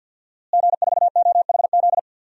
Morse code, heard as a keyed tone, about 46 wpm, 700 Hz. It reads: G4OHZ
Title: DXCC-100 Call Signs - 5 Characters - Repeated 3x - 50wpm